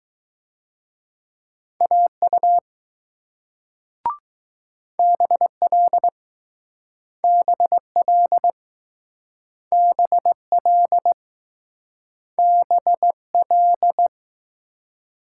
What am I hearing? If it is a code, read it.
Morse code, heard as a keyed tone, about 20 wpm, 700 Hz. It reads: AU BL BL BL BL